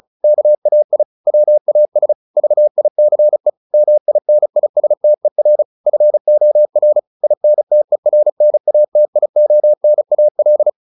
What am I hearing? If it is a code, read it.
Morse code, heard as a keyed tone, about 34 wpm, 600 Hz. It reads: KAI WAS VICE MINISTER FOR INTERNATIONAL